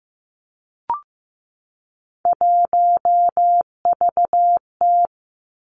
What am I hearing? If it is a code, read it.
Morse code, heard as a keyed tone, about 15 wpm, 700 Hz. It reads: 1VT
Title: Three Challenging Letter-Number Combinations 15wpm